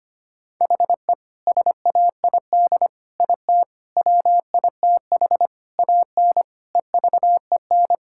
Morse code, heard as a keyed tone, about 25 wpm, 700 Hz. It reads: HE SAID IT WITH AN EVEN
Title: Sentences from Top 1600 Words 25wpm